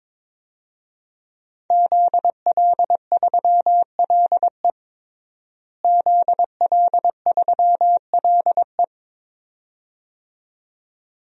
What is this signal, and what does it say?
Morse code, heard as a keyed tone, about 21 wpm, 700 Hz. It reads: ZL3LE ZL3LE
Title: DXCC-100 Call Signs - 5 Characters - Repeated 2x - 22wpm